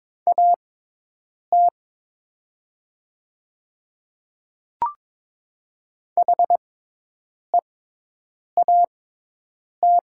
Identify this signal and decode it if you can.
Morse code, heard as a keyed tone, about 22 wpm, spaced as for 8 wpm, 700 Hz. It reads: AT HEAT
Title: 6x Character Spacing - Top 500 Words - 22wpm